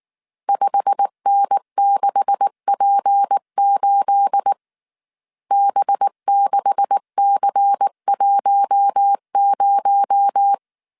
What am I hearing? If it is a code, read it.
Morse code, heard as a keyed tone, about 19 wpm, 800 Hz. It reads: 5N6P8 B6C10
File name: CW / DET / 1 9 1500C S A 18wpm 2459